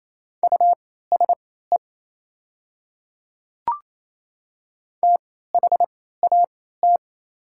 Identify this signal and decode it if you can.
Morse code, heard as a keyed tone, about 28 wpm, spaced as for 16 wpm, 700 Hz. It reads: USE THAT
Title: 3x Character Spacing - Top 100 Words - 28wpm